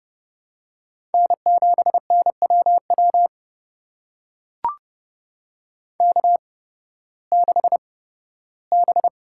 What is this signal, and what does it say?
Morse code, heard as a keyed tone, about 30 wpm, spaced as for 16 wpm, 700 Hz. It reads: N7NWW K6B
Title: USA Call Signs - 5 Characters - 8x Character Spacing - 30wpm